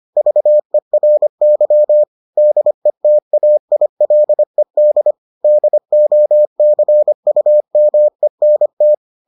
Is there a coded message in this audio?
Yes: Morse code from 0.2 to 8.9 s, heard as a keyed tone, about 25 wpm, 600 Hz. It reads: VERY DETAILED DOCUMENT